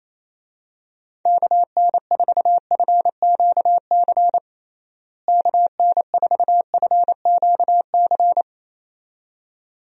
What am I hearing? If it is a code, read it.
Morse code, heard as a keyed tone, about 28 wpm, 700 Hz. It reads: KN4FQC KN4FQC
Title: USA Call Signs - 6 Characters - Repeated 2x - 28wpm